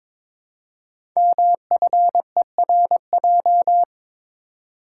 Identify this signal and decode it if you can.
Morse code, heard as a keyed tone, about 22 wpm, 700 Hz. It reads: MFERJ
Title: Code Groups - 5 Letters - 22wpm